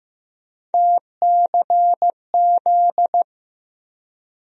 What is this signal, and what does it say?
Morse code, heard as a keyed tone, about 15 wpm, 700 Hz. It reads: TCZ
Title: USA Call Sign Postfixes - 3 Letters 15wpm